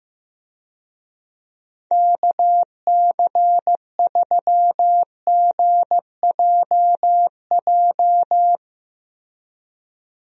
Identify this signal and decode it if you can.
Morse code, heard as a keyed tone, about 16 wpm, 700 Hz. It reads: KC3GJJ
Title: USA Call Signs - 6 Characters - 15wpm